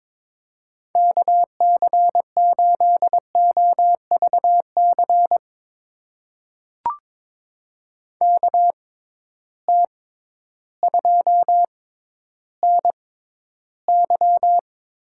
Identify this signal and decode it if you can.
Morse code, heard as a keyed tone, about 22 wpm, spaced as for 12 wpm, 700 Hz. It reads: KC8OVC KT2NY
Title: USA Call Signs - 6 Characters - 6x Character Spacing - 22wpm